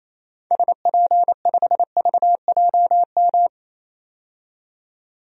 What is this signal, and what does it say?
Morse code, heard as a keyed tone, about 28 wpm, 700 Hz. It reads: SP5VJM
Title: DXCC-100 Call Signs - 6 Characters - 28wpm